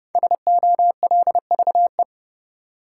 Morse code, heard as a keyed tone, about 29 wpm, 700 Hz. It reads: SOLVE
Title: Five Letter Words - 30wpm